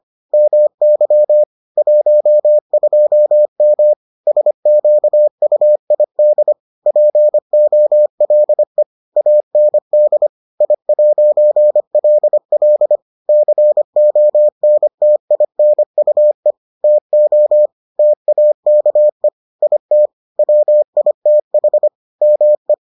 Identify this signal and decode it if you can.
Morse code, heard as a keyed tone, about 25 wpm, 600 Hz. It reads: MY 12M SQUID POLE AND I'LL CONTINUE TO TAKE IT WITH ME